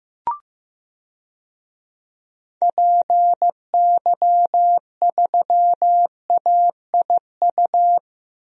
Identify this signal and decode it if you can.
Morse code, heard as a keyed tone, about 15 wpm, 700 Hz. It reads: PY3AIU
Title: DXCC-100 Call Signs - 6 Characters - Repeated 3x - 15wpm